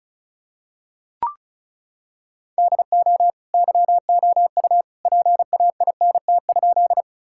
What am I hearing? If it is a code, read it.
Morse code, heard as a keyed tone, about 36 wpm, 700 Hz. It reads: DO YOU PAINT?
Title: Sentences from Top 1000 Words - Review Part 6 - 35wpm